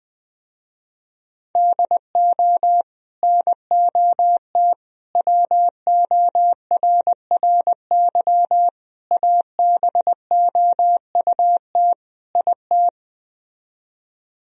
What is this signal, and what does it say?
Morse code, heard as a keyed tone, about 20 wpm, 700 Hz. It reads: DO NOT WORRY ABOUT IT
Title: Speed-Racing - Sets of 5 Words 17wpm